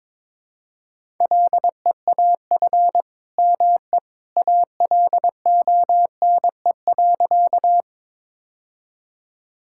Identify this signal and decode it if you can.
Morse code, heard as a keyed tone, about 22 wpm, 700 Hz. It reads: LEAF ME ALONE.
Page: morse-code-ninja-practice-22wpm